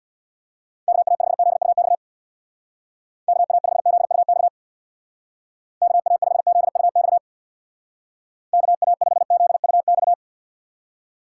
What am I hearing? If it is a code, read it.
Morse code, heard as a keyed tone, about 66 wpm, 700 Hz. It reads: KA5ZUX KA5ZUX KA5ZUX KA5ZUX